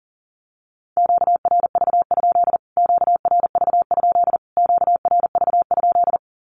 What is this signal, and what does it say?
Morse code, heard as a keyed tone, about 37 wpm, 700 Hz. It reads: QRV? QRV? QRV?
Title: Q-Codes Repeated 3x 40wpm